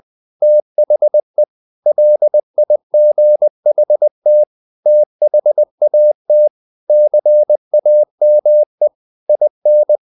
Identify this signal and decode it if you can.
Morse code, heard as a keyed tone, about 20 wpm, 600 Hz. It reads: THE LIGHT THAT CAME IN
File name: the old man and the sea / the old man and the sea part 1 20wpm